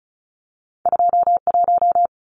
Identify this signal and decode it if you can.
Morse code, heard as a keyed tone, about 36 wpm, 700 Hz. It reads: 21
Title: Numbers - 2-Digit 35wpm